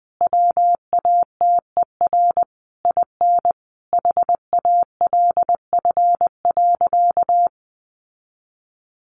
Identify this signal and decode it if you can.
Morse code, heard as a keyed tone, about 20 wpm, 700 Hz. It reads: WATER IN HALF.
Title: Sentences from Top 1100 Words 20wpm